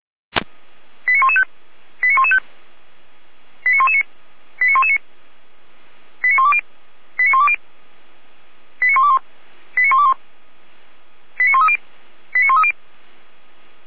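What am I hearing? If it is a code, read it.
Five-tone selective call (ZVEI), heard as quick runs of five tones, tones of about 70 ms. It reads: 89115 89115 89119 89119 89120 89120 89121 89121 89130 89130
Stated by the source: Five-tone ZVEI selective call, received on SDR.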